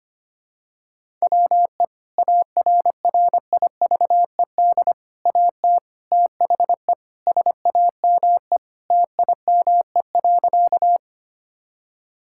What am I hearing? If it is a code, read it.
Morse code, heard as a keyed tone, about 25 wpm, 700 Hz. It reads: WE ARRIVED AT THE SAME TIME.
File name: practice / 25wpm / Sentences with 3-Word Phrases - 25wpm